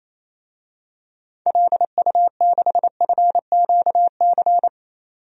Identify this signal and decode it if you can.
Morse code, heard as a keyed tone, about 27 wpm, 700 Hz. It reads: LU6FQC
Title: DXCC-100 Call Signs - 6 Characters - 28wpm